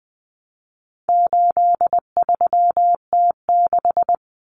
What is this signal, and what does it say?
Morse code, heard as a keyed tone, about 20 wpm, 700 Hz. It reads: 83T6
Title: Common Cut Numbers - 4-Digit 20wpm